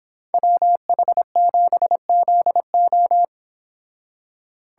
Morse code, heard as a keyed tone, about 25 wpm, 700 Hz. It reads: WH7ZO